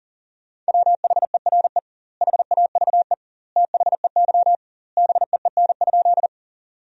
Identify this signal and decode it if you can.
Morse code, heard as a keyed tone, about 40 wpm, 700 Hz. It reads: WHERE HAVE THEY BEEN?
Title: 2x Word-Spacing - Sentences from Top 200 Words - 40wpm